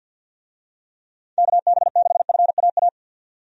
Morse code, heard as a keyed tone, about 51 wpm, 700 Hz. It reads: KB6FAA
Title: USA Call Signs - 6 Characters - 50wpm